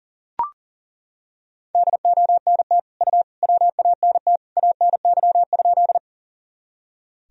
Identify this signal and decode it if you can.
Morse code, heard as a keyed tone, about 40 wpm, 700 Hz. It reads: DONT U WANT ANY?
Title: Sentences from Top 300 Words and Abbreviations 40wpm